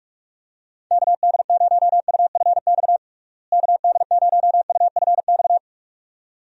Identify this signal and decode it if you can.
Morse code, heard as a keyed tone, about 41 wpm, 700 Hz. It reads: KD0UFX KD0UFX
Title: USA Call Signs - 6 Characters - Repeated 2x - 45wpm